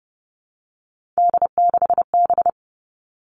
Morse code, heard as a keyed tone, about 30 wpm, 700 Hz. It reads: D6B
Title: Three Challenging Letter-Number Combinations 30wpm